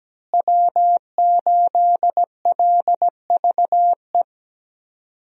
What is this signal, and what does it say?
Morse code, heard as a keyed tone, about 17 wpm, 700 Hz. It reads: W8LVE
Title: USA Call Signs - 5 Characters - Repeated 3x - 17wpm